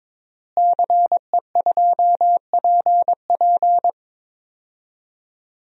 Morse code, heard as a keyed tone, about 22 wpm, 700 Hz. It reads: CE2PP
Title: DXCC-100 Call Signs - 5 Characters - Repeated 3x - 22wpm